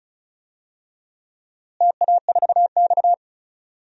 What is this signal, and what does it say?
Morse code, heard as a keyed tone, about 34 wpm, 700 Hz. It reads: TA4X
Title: DXCC-100 Call Signs - 4 Characters - Repeated 2x - 35wpm